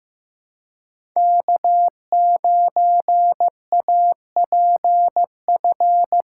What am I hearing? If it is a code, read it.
Morse code, heard as a keyed tone, about 18 wpm, 700 Hz. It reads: K9APF
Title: Speed Racing - USA 1x3 Call Signs 15wpm